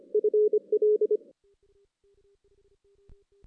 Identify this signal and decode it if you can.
Morse code, heard as a keyed tone, about 25 wpm, 420 Hz. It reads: FL
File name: 2016 JAN CW NAQP / 2016 Jan naqp k5ot - 0-5